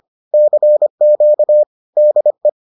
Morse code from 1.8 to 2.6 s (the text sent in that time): DE